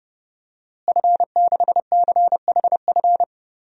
Morse code, heard as a keyed tone, about 30 wpm, 700 Hz. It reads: F6CHF
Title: DXCC-100 Call Signs - 5 Characters - Repeated 3x - 30wpm